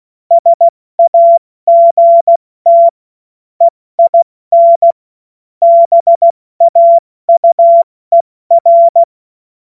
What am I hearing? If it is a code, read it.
Morse code, heard as a keyed tone, about 16 wpm, spaced as for 14 wpm, 680 Hz. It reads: SAGT EIN BAUER
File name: Morsekurs / Witze / Witz 02. Mai 2023 70 F80